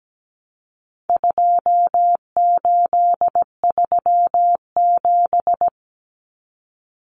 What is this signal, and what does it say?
Morse code, heard as a keyed tone, about 17 wpm, 700 Hz. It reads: 2837
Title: Common Cut Numbers - 4-Digit 17wpm